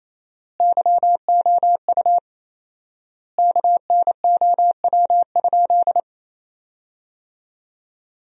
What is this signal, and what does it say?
Morse code, heard as a keyed tone, about 28 wpm, 700 Hz. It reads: YOU KNOW?
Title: 4x Word-Spacing - Sentences from Top 500 Words - 28wpm